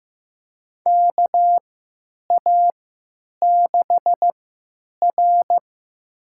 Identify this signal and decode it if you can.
Morse code, heard as a keyed tone, about 15 wpm, spaced as for 9 wpm, 700 Hz. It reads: KA6R
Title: USA Call Signs - 4 Characters - 3x Character Spacing - 15wpm